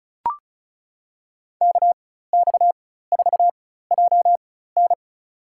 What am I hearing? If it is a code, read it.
Morse code, heard as a keyed tone, about 34 wpm, spaced as for 16 wpm, 700 Hz. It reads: KX4JN
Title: USA Call Signs - 5 Characters - 4x Character Spacing - 35wpm